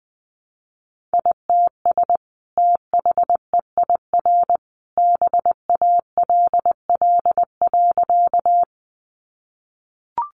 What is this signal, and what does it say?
Morse code, heard as a keyed tone, about 20 wpm, 700 Hz. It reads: ITS THEIR BALL.
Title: Sentences from Top 900 Words 20wpm